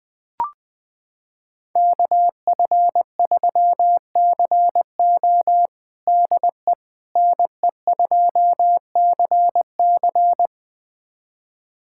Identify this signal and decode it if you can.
Morse code, heard as a keyed tone, about 20 wpm, 700 Hz. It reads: KF3CO DE NE2CC
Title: USA Callsign DE Callsign 20wpm